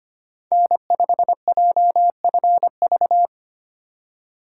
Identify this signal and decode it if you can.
Morse code, heard as a keyed tone, about 25 wpm, 700 Hz. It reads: N5JFV
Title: USA Call Signs - 4x Character Spacing - 25wpm